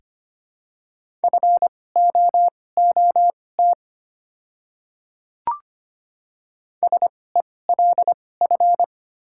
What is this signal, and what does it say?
Morse code, heard as a keyed tone, about 25 wpm, spaced as for 18 wpm, 700 Hz. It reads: FOOT SELF
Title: 2x Character Spacing - Top 500 Words - 25wpm